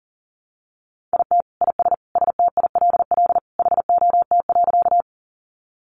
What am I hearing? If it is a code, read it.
Morse code, heard as a keyed tone, about 40 wpm, 700 Hz. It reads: IT IS STILL HOT.
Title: Easy Phrases 40wpm